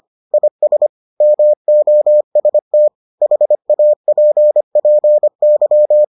Morse code, heard as a keyed tone, about 25 wpm, 600 Hz. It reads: IS MOST HAPPY